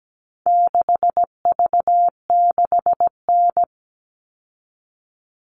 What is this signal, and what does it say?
Morse code, heard as a keyed tone, about 17 wpm, 700 Hz. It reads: 6V6N
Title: All Cut Numbers - 4-Digit 17wpm